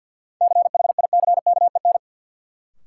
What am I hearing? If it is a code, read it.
Morse code, heard as a keyed tone, about 53 wpm, 700 Hz. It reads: CHICKEN
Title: Sentences from Top 1000 Words - Review Part 5 - New Word Drill - 50wpm